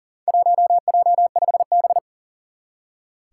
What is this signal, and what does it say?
Morse code, heard as a keyed tone, about 37 wpm, 700 Hz. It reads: 1J5B
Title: Four Challenging Letter-Number Combinations 40wpm